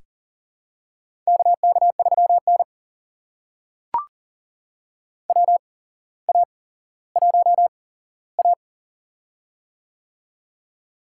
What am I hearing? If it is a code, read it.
Morse code, heard as a keyed tone, about 40 wpm, spaced as for 17 wpm, 700 Hz. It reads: KK3N WA1A
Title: USA Call Signs - 4 Characters - 8x Character Spacing - 40wpm